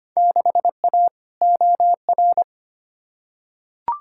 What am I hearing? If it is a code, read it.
Morse code, heard as a keyed tone, about 25 wpm, 700 Hz. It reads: 6A OR
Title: Field Day Exchanges - 25wpm